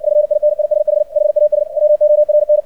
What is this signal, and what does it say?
Morse code, heard as a keyed tone, about 59 wpm, 600 Hz. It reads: FINISH HAS WORN
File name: archived / CRSnet